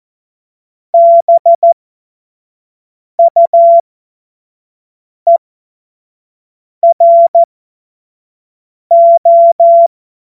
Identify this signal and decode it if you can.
Morse code, heard as a keyed tone, about 14 wpm, spaced as for 5 wpm, 680 Hz. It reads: BUERO
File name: Morsekurs / Witze / Witz 31. Januar 2023 25 F70